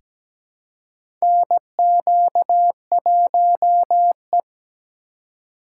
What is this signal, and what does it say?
Morse code, heard as a keyed tone, about 17 wpm, 700 Hz. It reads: NQ1E
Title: USA Call Signs - 4 Characters - 4x Character Spacing - 17wpm